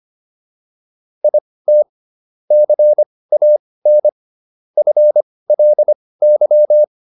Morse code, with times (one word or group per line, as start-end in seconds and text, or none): 1.2-1.8 IT
2.5-4.1 CAN
4.8-6.8 FLY